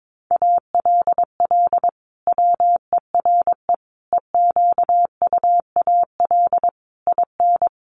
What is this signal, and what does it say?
Morse code, heard as a keyed tone, about 22 wpm, 700 Hz. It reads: ALL WERE EQUAL IN
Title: Sentences from Top 1400 Words 22wpm